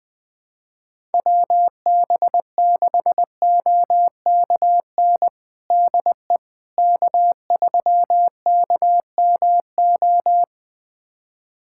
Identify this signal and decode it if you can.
Morse code, heard as a keyed tone, about 20 wpm, 700 Hz. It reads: WB6OKN DE K3KMO
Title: USA Callsign DE Callsign 20wpm